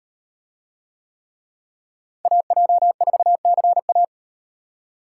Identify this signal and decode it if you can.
Morse code, heard as a keyed tone, about 38 wpm, 700 Hz. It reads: AJ4CA